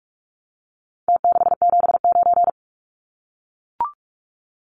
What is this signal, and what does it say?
Morse code, heard as a keyed tone, about 44 wpm, 700 Hz. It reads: T679
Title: Numbers - Contest Style - 4-Digit 45wpm